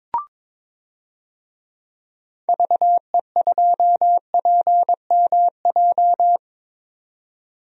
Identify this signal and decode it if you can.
Morse code, heard as a keyed tone, about 22 wpm, 700 Hz. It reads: VE2PMJ
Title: DXCC-100 Call Signs - 6 Characters - 22wpm